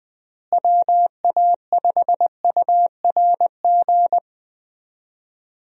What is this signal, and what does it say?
Morse code, heard as a keyed tone, about 20 wpm, 700 Hz. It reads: WA5URG
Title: USA Call Signs - 6 Characters - Repeated 2x - 20wpm